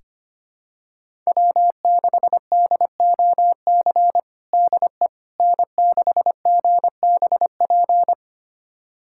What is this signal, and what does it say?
Morse code, heard as a keyed tone, about 25 wpm, 700 Hz. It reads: W6DOC DE N6GBP
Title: USA Callsign DE Callsign - 5 Characters - 25wpm